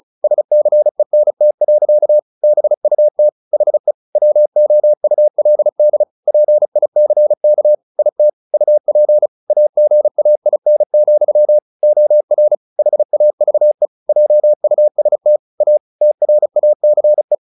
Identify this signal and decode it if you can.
Morse code, heard as a keyed tone, about 34 wpm, 600 Hz. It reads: SCENT. BUT HE WOULD PICK IT UP AGAIN, OR HAVE JUST A TRACE
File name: the old man and the sea 2 35wpm